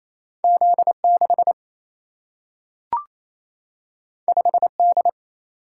Morse code, heard as a keyed tone, about 28 wpm, 700 Hz. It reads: Z 6 5 D
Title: Single-Letter Single-Number 28wpm